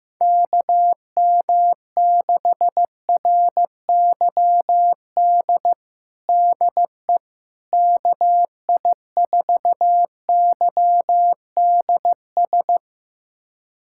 Morse code, heard as a keyed tone, about 15 wpm, 700 Hz. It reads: KM6RYD DE KI4YDS
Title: USA Callsign DE Callsign - 6 Characters - 15wpm